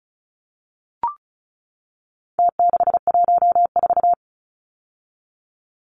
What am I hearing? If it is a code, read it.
Morse code, heard as a keyed tone, about 34 wpm, 700 Hz. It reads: T614
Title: Numbers - Contest Style - 4-Digit 35wpm